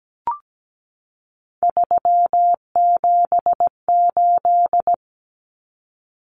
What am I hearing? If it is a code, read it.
Morse code, heard as a keyed tone, about 17 wpm, 700 Hz. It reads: 378
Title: Common Cut Numbers - 3-Digit 17wpm